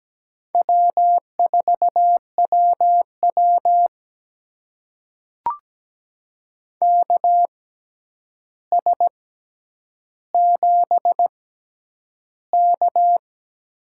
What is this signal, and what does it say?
Morse code, heard as a keyed tone, about 17 wpm, spaced as for 9 wpm, 700 Hz. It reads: W4WW KS7K
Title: USA Call Signs - 4 Characters - 6x Character Spacing - 17wpm